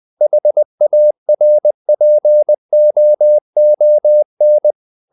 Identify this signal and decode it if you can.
Morse code, heard as a keyed tone, about 20 wpm, 600 Hz. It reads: HARPOON
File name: the old man and the sea part 2 20wpm